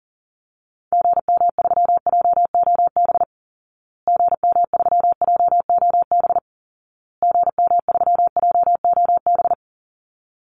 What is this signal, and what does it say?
Morse code, heard as a keyed tone, about 40 wpm, 700 Hz. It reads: GM3JOB GM3JOB GM3JOB
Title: Call Signs Repeated 2x - 40wpm